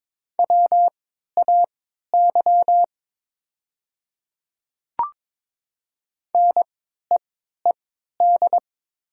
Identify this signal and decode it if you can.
Morse code, heard as a keyed tone, about 22 wpm, spaced as for 13 wpm, 700 Hz. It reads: WAY NEED
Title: 3x Character Spacing - Top 200 Words - 22wpm